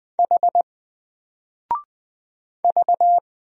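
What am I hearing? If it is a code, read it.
Morse code, heard as a keyed tone, about 20 wpm, 700 Hz. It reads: H V
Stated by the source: Single letter and number Morse drill at 20 wpm, from N4LCD.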